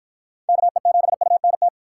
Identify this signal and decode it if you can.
Morse code, heard as a keyed tone, about 50 wpm, 700 Hz. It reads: KE7UNT